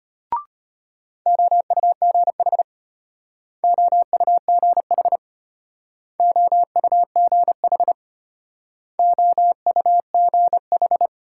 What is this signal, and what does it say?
Morse code, heard as a keyed tone, about 33 wpm, 700 Hz. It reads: OUGH OUGH OUGH OUGH